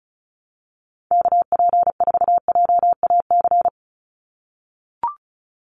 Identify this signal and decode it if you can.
Morse code, heard as a keyed tone, about 34 wpm, 700 Hz. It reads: KP4JAC
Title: USA Call Signs 35wpm